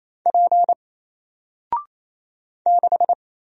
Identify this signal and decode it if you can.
Morse code, heard as a keyed tone, about 27 wpm, 700 Hz. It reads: P 6
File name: Single-Letter-Number-28wpm